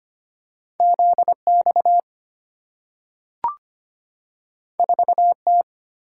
Morse code, heard as a keyed tone, about 25 wpm, 690 Hz. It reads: ZX 4T
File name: Two Letter-Number Combinations - 25wpm